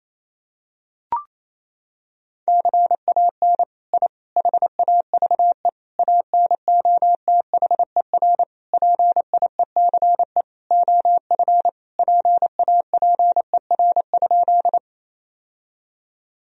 Morse code, heard as a keyed tone, about 27 wpm, 700 Hz. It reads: CAN I HAVE ANOTHER PIECE OF PAPER?
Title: Sentences from Top 1000 Words - Review Part 2 - 28wpm